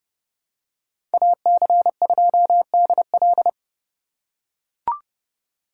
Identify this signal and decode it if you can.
Morse code, heard as a keyed tone, about 29 wpm, 700 Hz. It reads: AC2DL